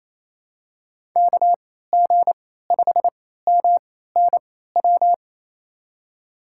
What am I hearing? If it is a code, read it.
Morse code, heard as a keyed tone, about 27 wpm, spaced as for 16 wpm, 700 Hz. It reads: KG5MNW
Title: USA Call Signs - 3x Character Spacing - 28wpm